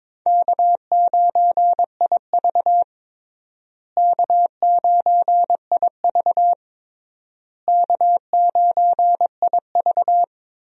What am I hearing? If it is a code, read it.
Morse code, heard as a keyed tone, about 22 wpm, 700 Hz. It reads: K9IV K9IV K9IV
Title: USA Call Signs - 4 Characters - Repeated 3x - 22wpm